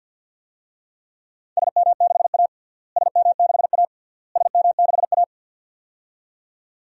Morse code, heard as a keyed tone, about 52 wpm, 700 Hz. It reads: SM6A SM6A SM6A
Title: DXCC-100 Call Signs - 4 Characters - Repeated 3x - 50wpm